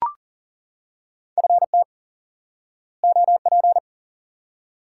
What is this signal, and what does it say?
Morse code, heard as a keyed tone, about 40 wpm, 700 Hz. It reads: FT OP
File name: IQE Course 3 - Lesson 89 - YAGI - Two Elements - 40wpm